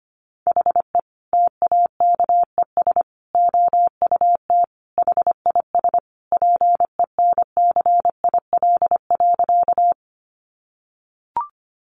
Morse code, heard as a keyed tone, about 25 wpm, 700 Hz. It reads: HE TAKES OUT HIS PENCIL.
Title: Additional Easy Phrases 25wpm